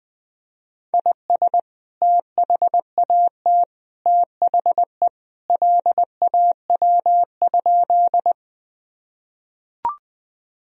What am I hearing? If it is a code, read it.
Morse code, heard as a keyed tone, about 20 wpm, 700 Hz. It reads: IS THAT THE LAW?
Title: Sentences from Top 1000 Words - Review Part 3 - 20wpm